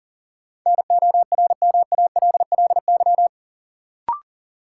Morse code, heard as a keyed tone, about 40 wpm, 700 Hz. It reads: NORMALLY
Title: Sentences from Top 1000 Words - Review Part 6 - New Word Drill - 40wpm